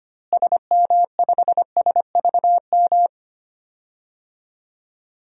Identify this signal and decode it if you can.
Morse code, heard as a keyed tone, about 25 wpm, 700 Hz. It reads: SM5SVM
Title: DXCC-100 Call Signs - 6 Characters - Repeated 2x - 25wpm